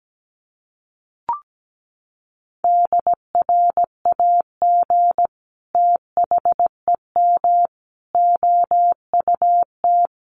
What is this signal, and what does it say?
Morse code, heard as a keyed tone, about 17 wpm, 700 Hz. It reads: DRAG THEM OUT
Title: Sentences from Top 1500 Words 17wpm